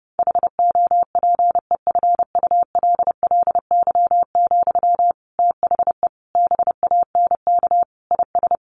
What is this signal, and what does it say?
Morse code, heard as a keyed tone, about 28 wpm, 700 Hz. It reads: HOPEFULLY, THE BANK IS
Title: Sentences from Top 1500 Words 30wpm